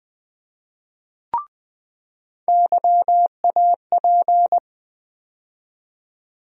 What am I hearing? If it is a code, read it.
Morse code, heard as a keyed tone, about 20 wpm, 700 Hz. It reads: YAP
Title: USA Call Sign Postfixes - 3 Letters 20wpm